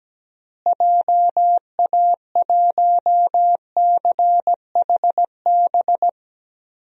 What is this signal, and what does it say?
Morse code, heard as a keyed tone, about 17 wpm, 700 Hz. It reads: JA1CHB
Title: DXCC-100 Call Signs - 6 Characters - Repeated 3x - 17wpm